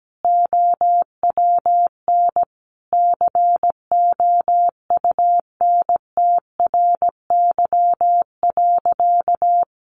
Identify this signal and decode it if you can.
Morse code, heard as a keyed tone, about 17 wpm, 700 Hz. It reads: OWN COUNTRY.
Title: Sentences from Top 300 Words 17wpm